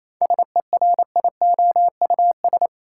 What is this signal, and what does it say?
Morse code, heard as a keyed tone, about 28 wpm, 700 Hz. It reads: SERIOUS
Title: Seven Letter Words - 28wpm